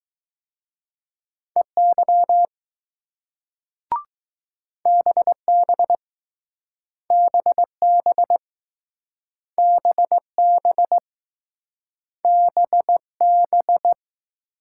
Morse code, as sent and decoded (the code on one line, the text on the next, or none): . -.-- / -... -... / -... -... / -... -... / -... -...
EY BB BB BB BB